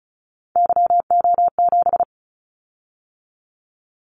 Morse code, heard as a keyed tone, about 36 wpm, 700 Hz. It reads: YO7
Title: Callsign Prefixes for Top 25 DXCC - 35wpm